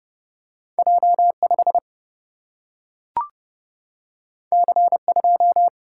Morse code, heard as a keyed tone, about 30 wpm, 700 Hz. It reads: J5 C2
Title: Single-Letter Single-Number 30wpm